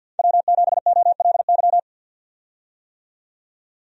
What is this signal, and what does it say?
Morse code, heard as a keyed tone, about 51 wpm, 700 Hz. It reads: W7OLY
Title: USA Call Signs - 5 Characters - 50wpm